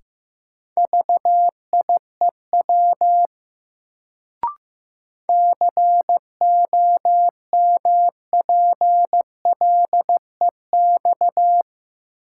Morse code, heard as a keyed tone, about 15 wpm, 700 Hz. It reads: VIEW COMPLEX
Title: Sentences from Top 900 Words - New Word Drill - 15wpm